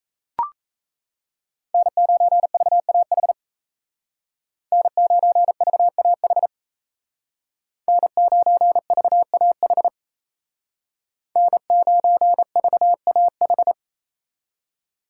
Morse code, heard as a keyed tone, about 37 wpm, 700 Hz. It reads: N9VAH N9VAH N9VAH N9VAH